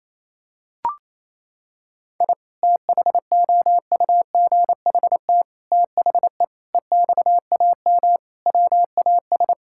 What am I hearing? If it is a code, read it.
Morse code, heard as a keyed tone, about 28 wpm, 700 Hz. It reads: I THOUGHT THE EXAM WAS
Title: Sentences from Top 1000 Words - Review Part 5 - 28wpm